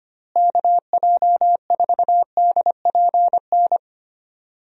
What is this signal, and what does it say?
Morse code, heard as a keyed tone, about 25 wpm, 700 Hz. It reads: KJ4DPN
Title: USA Call Signs - 6 Characters - Repeated 3x - 25wpm